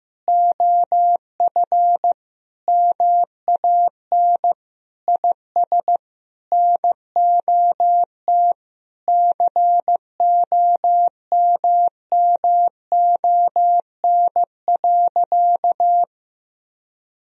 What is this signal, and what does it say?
Morse code, heard as a keyed tone, about 15 wpm, 700 Hz. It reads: OF MAN IS NOT COMMON.
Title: Sentences from Top 1000 Words - Review Part 3 - 15wpm